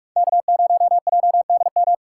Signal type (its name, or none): Morse code